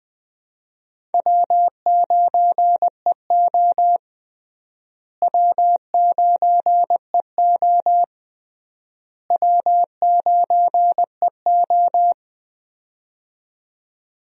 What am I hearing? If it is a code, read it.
Morse code, heard as a keyed tone, about 21 wpm, 700 Hz. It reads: W9EO W9EO W9EO
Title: USA Call Signs - 4 Characters - Repeated 3x - 20wpm